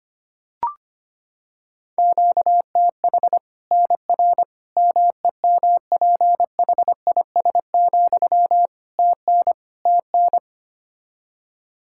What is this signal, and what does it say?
Morse code, heard as a keyed tone, about 25 wpm, 700 Hz. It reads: QTH NR MEMPHIS, TN TN
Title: Practice QSO - Exchange 1 - Beginning - 25wpm